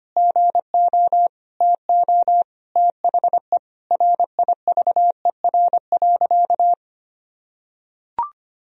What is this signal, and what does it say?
Morse code, heard as a keyed tone, about 25 wpm, 700 Hz. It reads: GO TO THE RIVER.
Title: Sentences from Top 1000 Words - Review Part 4 - 25wpm